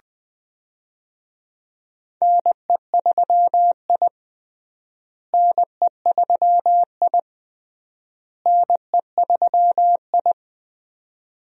Morse code, heard as a keyed tone, about 20 wpm, 700 Hz. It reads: NE3I NE3I NE3I
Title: USA Call Signs - 4 Characters - Repeated 3x - 20wpm